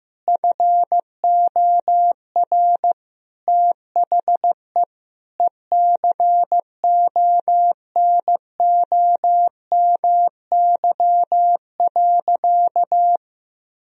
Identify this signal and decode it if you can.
Morse code, heard as a keyed tone, about 15 wpm, 700 Hz. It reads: FOR THE ECONOMY.
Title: Sentences from Top 1000 Words - Review Part 4 - 15wpm